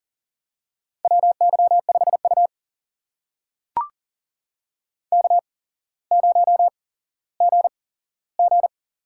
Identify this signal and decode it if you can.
Morse code, heard as a keyed tone, about 38 wpm, spaced as for 17 wpm, 700 Hz. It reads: WY5U K0GG
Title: USA Call Signs - 4 Characters - 8x Character Spacing - 40wpm